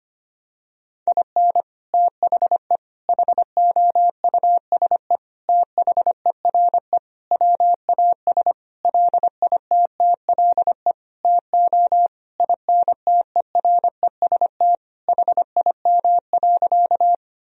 Morse code, heard as a keyed tone, about 25 wpm, 700 Hz. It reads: IN THE HOUSE THERE WAS LITTLE TO INTEREST HIM.
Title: Sentences from Top 1000 Words - Review Part 2 - 25wpm